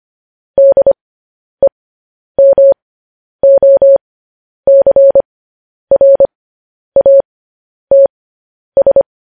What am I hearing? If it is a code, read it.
Morse code, heard as a keyed tone, about 25 wpm, spaced as for 10 wpm, 550 Hz. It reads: DEMOCRATS